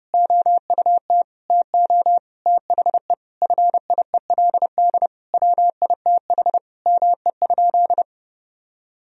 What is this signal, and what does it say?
Morse code, heard as a keyed tone, about 30 wpm, 700 Hz. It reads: OUT TO THE FIELD WITH ME?
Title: Sentences from Top 1000 Words - Review Part 2 - 30wpm